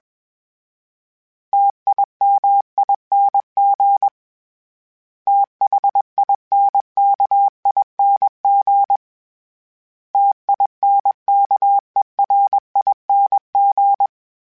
Morse code, heard as a keyed tone, about 21 wpm, 800 Hz. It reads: TIMING THINKING TINKERING